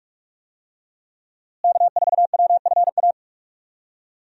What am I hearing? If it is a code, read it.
Morse code, heard as a keyed tone, about 41 wpm, 700 Hz. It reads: K4WFA
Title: USA Call Signs - 5 Characters - Repeated 2x - 45wpm